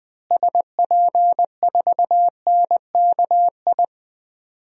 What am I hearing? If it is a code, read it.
Morse code, heard as a keyed tone, about 20 wpm, 700 Hz. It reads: SP4NKI
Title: DXCC-100 Call Signs - 6 Characters - Repeated 2x - 20wpm